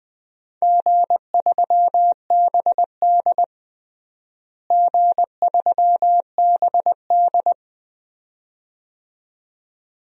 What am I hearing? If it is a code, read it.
Morse code, heard as a keyed tone, about 20 wpm, 700 Hz. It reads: G3BD G3BD
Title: DXCC-100 Call Signs - 4 Characters - Repeated 2x - 20wpm